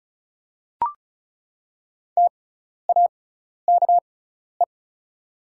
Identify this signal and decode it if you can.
Morse code, heard as a keyed tone, about 36 wpm, spaced as for 12 wpm, 700 Hz. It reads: TAKE
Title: 6x Character Spacing - Top 200 Words - 35wpm